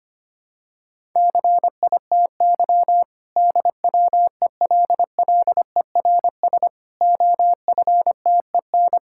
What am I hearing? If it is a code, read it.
Morse code, heard as a keyed tone, about 25 wpm, 700 Hz. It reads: CITY DWELLERS OFTEN
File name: practice / 25wpm / Difficult Sentences with Hints - 25wpm